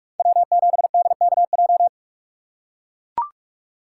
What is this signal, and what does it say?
Morse code, heard as a keyed tone, about 44 wpm, 700 Hz. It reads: W7DKJ